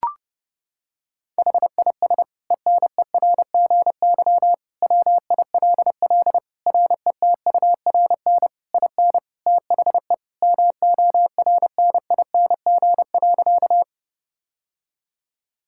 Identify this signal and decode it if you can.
Morse code, heard as a keyed tone, about 30 wpm, 700 Hz. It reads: HIS ENERGY WILL RETURN IN THE MORNING.
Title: Sentences from Top 1000 Words - Review Part 6 - 30wpm